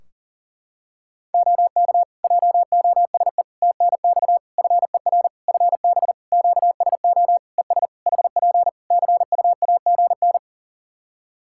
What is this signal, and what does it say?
Morse code, heard as a keyed tone, about 40 wpm, 700 Hz. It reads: OK JOSE TNX FER FB QSO ES HP CUAGN
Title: K1USN SST QSO Practice 40wpm